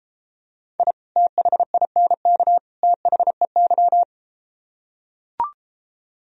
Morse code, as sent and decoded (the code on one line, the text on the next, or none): .. / - .... .. -. -.- / - .... -.--
I THINK THEY